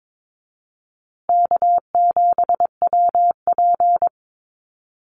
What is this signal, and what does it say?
Morse code, heard as a keyed tone, about 22 wpm, 700 Hz. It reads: K7WP
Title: Easy Call Signs Repeated 2x - 22wpm